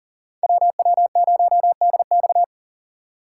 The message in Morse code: WW0DX

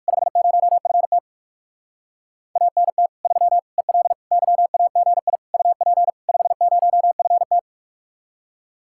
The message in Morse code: 50FT ANT 3 EL YAGI UP 50FT

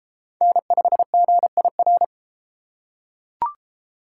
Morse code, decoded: N5GIR

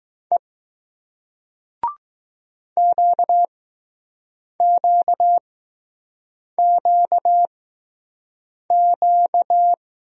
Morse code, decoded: E QQQQ

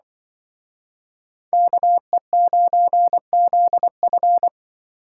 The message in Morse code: KE9ZF

